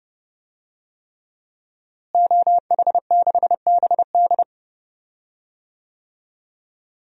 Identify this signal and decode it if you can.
Morse code, heard as a keyed tone, about 30 wpm, 700 Hz. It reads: OH6BD
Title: DXCC-100 Call Signs - 5 Characters - Repeated 3x - 30wpm